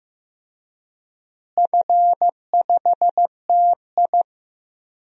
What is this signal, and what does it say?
Morse code, heard as a keyed tone, about 16 wpm, 700 Hz. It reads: F5TI
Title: DXCC-100 Call Signs - 4 Characters - Repeated 3x - 15wpm